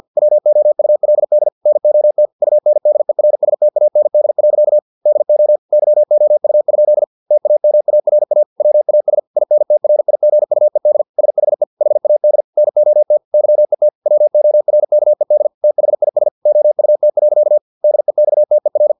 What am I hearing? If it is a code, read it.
Morse code, heard as a keyed tone, about 50 wpm, 600 Hz. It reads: WOULD NOT UNDERSTAND. DO YOU? TAMARA WAS INTRIGUED SHE HAD NOT YET WORKED THIS OUT. DEXTER
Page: Never 49wpm